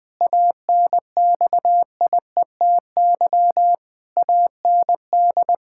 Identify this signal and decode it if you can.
Morse code, heard as a keyed tone, about 20 wpm, 700 Hz. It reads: ANXIETY AND
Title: Pangram Challenge v2 - 20wpm